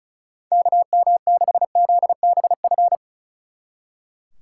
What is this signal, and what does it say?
Morse code, heard as a keyed tone, about 45 wpm, 700 Hz. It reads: KM6ZBF